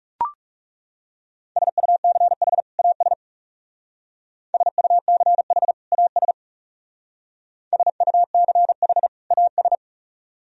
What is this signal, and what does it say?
Morse code, heard as a keyed tone, about 37 wpm, 700 Hz. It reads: SUCH AS SUCH AS SUCH AS
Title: Speed-Racing - Sets of 2 Words 30wpm